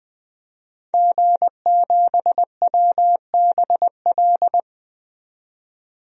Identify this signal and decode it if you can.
Morse code, heard as a keyed tone, about 20 wpm, 700 Hz. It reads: G7WBL
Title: DXCC-100 Call Signs - 5 Characters - 20wpm